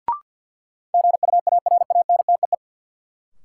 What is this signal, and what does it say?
Morse code, heard as a keyed tone, about 49 wpm, 700 Hz. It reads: GUARANTEE
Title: Sentences from Top 1600 Words - New Word Drill - 50wpm